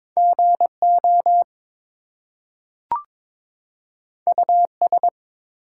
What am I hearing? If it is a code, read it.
Morse code, heard as a keyed tone, about 22 wpm, 700 Hz. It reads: GO US
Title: Two Letter Words - 22wpm